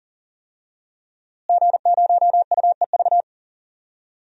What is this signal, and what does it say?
Morse code, heard as a keyed tone, about 40 wpm, 700 Hz. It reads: G0UEV
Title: DXCC-100 Call Signs - 5 Characters - Repeated 3x - 40wpm